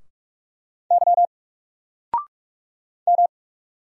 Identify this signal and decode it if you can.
Morse code, heard as a keyed tone, about 46 wpm, 700 Hz. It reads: Y M
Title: Single Letter-Number 45wpm